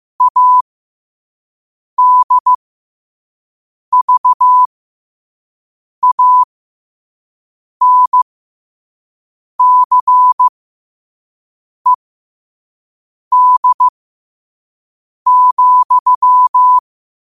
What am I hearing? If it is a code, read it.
Morse code, heard as a keyed tone, about 15 wpm, spaced as for 5 wpm, 1000 Hz. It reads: ADVANCED,
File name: qotdfiles / 5wpm